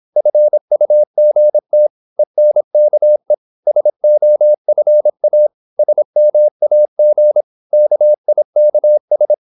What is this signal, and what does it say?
Morse code, heard as a keyed tone, about 26 wpm, 600 Hz. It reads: FUGT ENKE SOFA SMAG KIKS
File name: Korte ord DK 26wpm